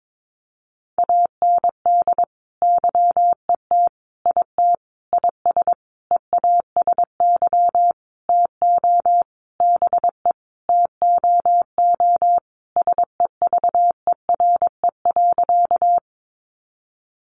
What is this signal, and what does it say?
Morse code, heard as a keyed tone, about 22 wpm, 700 Hz. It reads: AND YET IT IS EASY TO BE TOO SEVERE.
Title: Sentences from Top 1400 Words 22wpm